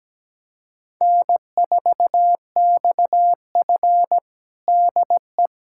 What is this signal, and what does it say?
Morse code, heard as a keyed tone, about 17 wpm, 700 Hz. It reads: N4XF DE
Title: USA Callsign DE Callsign - 4 Characters - 17wpm